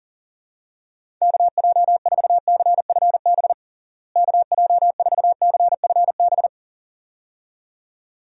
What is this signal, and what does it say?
Morse code, heard as a keyed tone, about 40 wpm, 700 Hz. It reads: KJ4CFB KJ4CFB
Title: USA Call Signs - 6 Characters - Repeated 2x - 40wpm